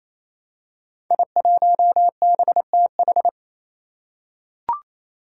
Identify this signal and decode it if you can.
Morse code, heard as a keyed tone, about 27 wpm, 700 Hz. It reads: I1BTH
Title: Five Challenging Letter-Number Combinations 28wpm